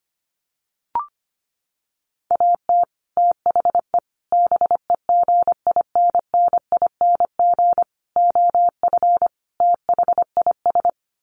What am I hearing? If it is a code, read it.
Morse code, heard as a keyed tone, about 25 wpm, 700 Hz. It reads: AT THE BEGINNING OF THIS
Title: Sets of 5 Words Encore 25wpm